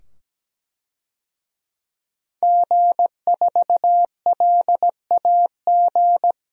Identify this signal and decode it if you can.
Morse code, heard as a keyed tone, about 17 wpm, 700 Hz. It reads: G4LAG
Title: DXCC-100 Call Signs - 5 Characters - 17wpm